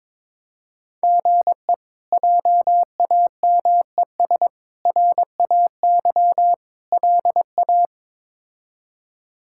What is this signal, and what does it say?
Morse code, heard as a keyed tone, about 21 wpm, 700 Hz. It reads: GE JAMES RAY LA
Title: K1USN SST QSO Practice - 22wpm